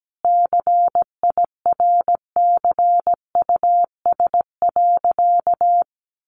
Morse code, heard as a keyed tone, about 17 wpm, 700 Hz. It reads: CIRCUS.
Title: Easy Phrases 17wpm